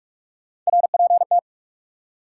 Morse code, heard as a keyed tone, about 62 wpm, spaced as for 53 wpm, 700 Hz. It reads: RPT
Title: Speed-Racing Top 180 QSO Elements 45wpm